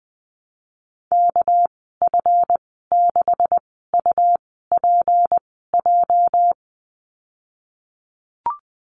Morse code, heard as a keyed tone, about 20 wpm, spaced as for 14 wpm, 700 Hz. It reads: KF6UPJ
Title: 213 Lesson 34g - Number 8 - Call Signs 20wpm